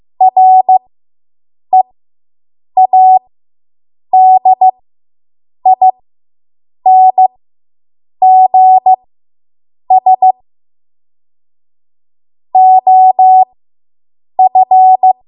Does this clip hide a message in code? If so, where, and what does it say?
Yes: Morse code from 0.2 to 15.1 s, heard as a keyed tone, about 15 wpm, spaced as for 7 wpm, 750 Hz. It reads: READINGS OF